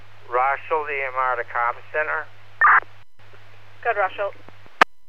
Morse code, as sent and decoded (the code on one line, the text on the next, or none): none